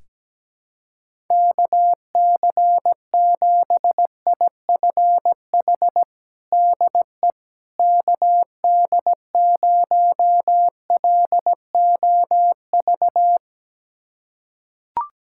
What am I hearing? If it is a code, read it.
Morse code, heard as a keyed tone, about 17 wpm, 700 Hz. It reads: KC7IFH DE KD0LOV
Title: USA Callsign DE Callsign 17wpm